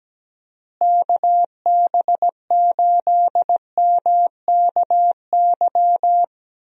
Morse code, heard as a keyed tone, about 17 wpm, 700 Hz. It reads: KB8MKY
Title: USA Call Signs - 6 Characters - Repeated 3x - 17wpm